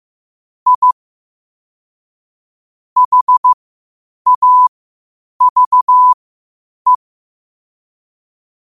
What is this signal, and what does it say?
Morse code, heard as a keyed tone, about 15 wpm, spaced as for 8 wpm, 1000 Hz. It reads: I HAVE
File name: qotdfiles / 8wpm